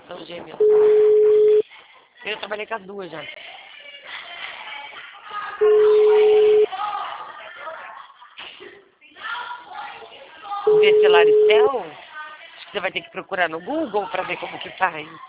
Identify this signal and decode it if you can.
Ringback tone (Europe, 425 Hz), with speech.